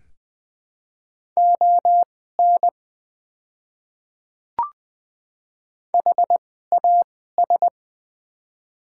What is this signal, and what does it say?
Morse code, heard as a keyed tone, about 20 wpm, spaced as for 15 wpm, 700 Hz. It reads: ON HAS
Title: 2x Character Spacing - Top 200 Words - 20wpm